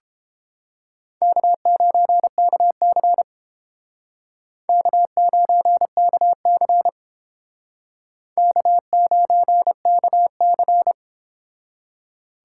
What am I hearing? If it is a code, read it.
Morse code, heard as a keyed tone, about 30 wpm, 700 Hz. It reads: K9KC K9KC K9KC